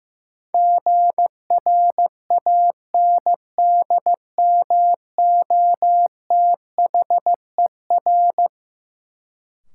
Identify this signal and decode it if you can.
Morse code, heard as a keyed tone, about 15 wpm, 700 Hz. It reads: GRANDMOTHER
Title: Sentences from Top 1200 Words - New Word Drill Alt - 15wpm